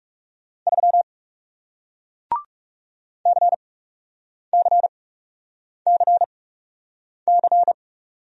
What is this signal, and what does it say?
Morse code, heard as a keyed tone, about 41 wpm, spaced as for 8 wpm, 700 Hz. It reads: 3 CCCC